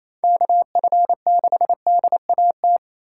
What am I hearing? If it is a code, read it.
Morse code, heard as a keyed tone, about 28 wpm, 700 Hz. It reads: KF6DAT
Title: USA Call Signs - 6 Characters - 2x Character Spacing - 28wpm